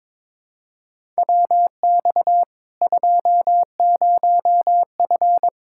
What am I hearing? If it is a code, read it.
Morse code, heard as a keyed tone, about 22 wpm, 700 Hz. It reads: WX 20F
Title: Practice QSO - Exchange 2 - Info - 22wpm